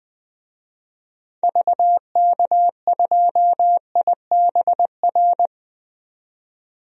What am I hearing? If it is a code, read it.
Morse code, heard as a keyed tone, about 20 wpm, 700 Hz. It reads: VK2IBR
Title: DXCC-100 Call Signs - 6 Characters - 20wpm